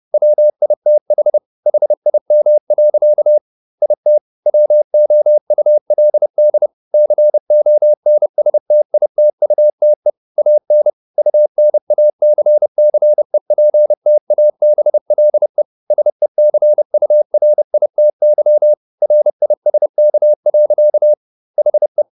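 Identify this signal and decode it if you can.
Morse code, heard as a keyed tone, about 30 wpm, 600 Hz. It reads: WITH HIM. IT WOULD CONSTITUTE AN UNACCEPTABLE SECURITY RISK. HE
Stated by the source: Morse code at 30 wpm, the novel Never by Ken Follett in English, from CW med Gnister.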